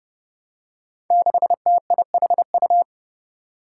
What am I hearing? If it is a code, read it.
Morse code, heard as a keyed tone, about 28 wpm, 700 Hz. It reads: 6TIHU
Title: Five Challenging Letter-Number Combinations 30wpm